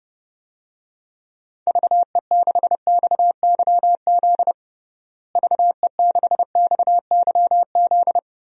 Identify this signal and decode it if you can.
Morse code, heard as a keyed tone, about 30 wpm, 700 Hz. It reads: VE6XYZ VE6XYZ
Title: DXCC-100 Call Signs - 6 Characters - Repeated 3x - 30wpm